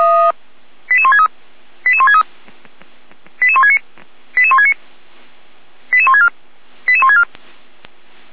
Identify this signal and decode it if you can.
Five-tone selective call (ZVEI), heard as quick runs of five tones, tones of about 70 ms. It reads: 88163 88163 88169 88169 88164 88164